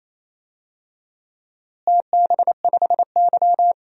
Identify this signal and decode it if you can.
Morse code, heard as a keyed tone, about 28 wpm, 700 Hz. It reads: TB5Y
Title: DXCC-100 Call Signs - 4 Characters - Repeated 3x - 28wpm